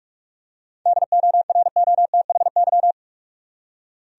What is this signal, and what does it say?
Morse code, heard as a keyed tone, about 44 wpm, 700 Hz. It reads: DOROTHY
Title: USA Common Names - 45wpm